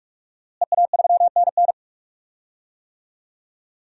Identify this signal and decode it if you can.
Morse code, heard as a keyed tone, about 42 wpm, 700 Hz. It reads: EA3NN
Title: DXCC-100 Call Signs - 5 Characters - Repeated 2x - 45wpm